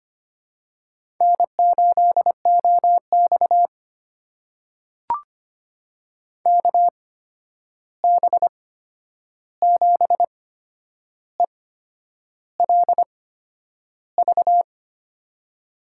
Morse code, heard as a keyed tone, about 25 wpm, spaced as for 9 wpm, 700 Hz. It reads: N8OX KB7ELV
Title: USA Call Signs - 8x Character Spacing - 25wpm